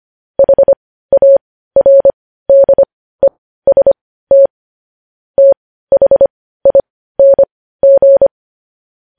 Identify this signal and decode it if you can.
Morse code, heard as a keyed tone, about 25 wpm, spaced as for 15 wpm, 550 Hz. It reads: HARDEST THING
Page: News Headlines in Morse Code at 15 WPM